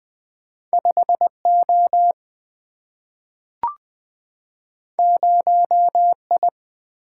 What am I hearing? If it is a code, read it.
Morse code, heard as a keyed tone, about 20 wpm, 700 Hz. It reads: 5 O 0 I
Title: Single-Letter Single-Number 20wpm